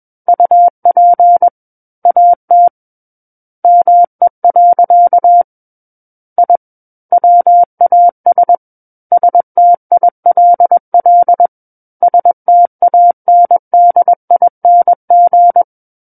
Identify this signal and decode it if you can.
Morse code, heard as a keyed tone, about 21 wpm, 700 Hz. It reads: UP AT ME. I WAS STILL STANDING
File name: Book One / 12-Chapter 12 - 21 WPM